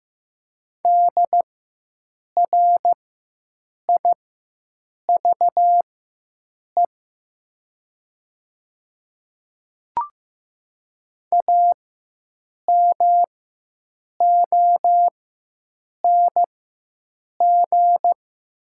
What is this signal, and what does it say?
Morse code, heard as a keyed tone, about 15 wpm, spaced as for 7 wpm, 700 Hz. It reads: DRIVE AMONG
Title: 4x Character Spacing - Top 500 Words - 15wpm